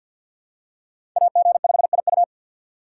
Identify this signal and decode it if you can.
Morse code, heard as a keyed tone, about 53 wpm, 700 Hz. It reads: AG5IU
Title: USA Call Signs - 5 Characters - 8x Character Spacing - 50wpm